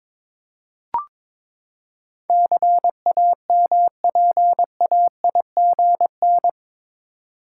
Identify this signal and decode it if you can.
Morse code, heard as a keyed tone, about 22 wpm, 700 Hz. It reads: CAMPAIGN
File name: practice / 22wpm / Eight Letter Words - 22wpm